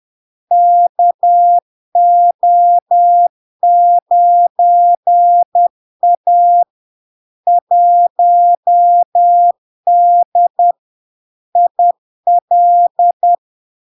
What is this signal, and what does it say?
Morse code, heard as a keyed tone, about 10 wpm, 700 Hz. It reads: KO9A 1D IL
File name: LARK-FD-2022-Log-in-CW-10wpm-700-Hz